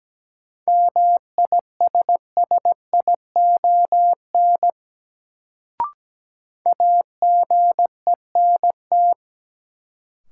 Morse code, heard as a keyed tone, about 17 wpm, 700 Hz. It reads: MISSION AGENT
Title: Sentences from Top 1600 Words - New Word Drill - 17wpm